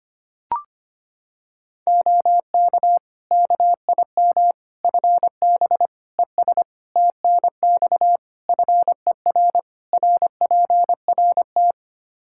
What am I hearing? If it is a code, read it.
Morse code, heard as a keyed tone, about 25 wpm, 700 Hz. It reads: OK KIM FB ES TNX FER RPRT
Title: Practice QSO - Exchange 2 - Info - 25wpm